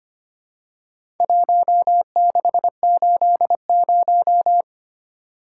Morse code, heard as a keyed tone, about 25 wpm, 700 Hz. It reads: 1680
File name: practice / 25wpm / Four Letter-Number Combinations - 25wpm